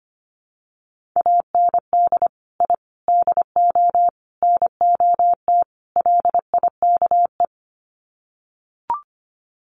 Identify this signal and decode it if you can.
Morse code, heard as a keyed tone, about 25 wpm, 700 Hz. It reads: AND I DO NOT LIKE
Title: Sets of 5 Words Encore2 25wpm